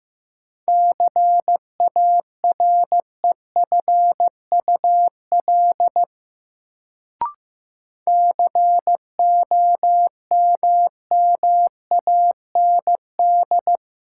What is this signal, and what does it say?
Morse code, heard as a keyed tone, about 15 wpm, 700 Hz. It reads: CAREFUL COMMAND
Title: Sentences from Top 1600 Words - New Word Drill Alt - 15wpm